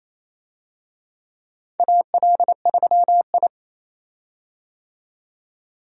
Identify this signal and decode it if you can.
Morse code, heard as a keyed tone, about 28 wpm, 700 Hz. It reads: AL3I
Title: USA Call Signs - 4 Characters - 28wpm